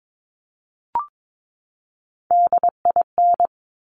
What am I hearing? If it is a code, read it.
Morse code, heard as a keyed tone, about 22 wpm, 700 Hz. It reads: DIN
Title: Top Three Letter Combinations in Words 22wpm